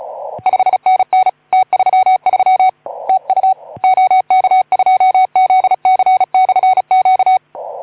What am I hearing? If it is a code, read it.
Morse code, heard as a keyed tone, about 35 wpm, 750 Hz. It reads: 5NN T33 TU OK2ZC/Q
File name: Archiv / CUC181 / OK2ZC CUC 181